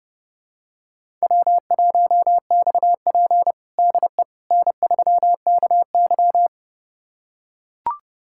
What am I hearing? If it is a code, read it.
Morse code, heard as a keyed tone, about 30 wpm, 700 Hz. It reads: W1XP DE N3KY
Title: USA Callsign DE Callsign - 4 Characters - 30wpm